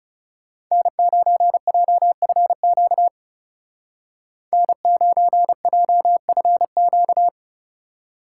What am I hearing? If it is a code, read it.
Morse code, heard as a keyed tone, about 36 wpm, 700 Hz. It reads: N9JFQ N9JFQ
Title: Speed Racing - USA 1x3 Call Signs 30wpm